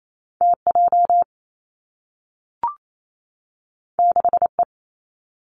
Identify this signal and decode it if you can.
Morse code, heard as a keyed tone, about 28 wpm, 700 Hz. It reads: T J 6 E
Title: Two Challenging Letter-Number Combinations 28wpm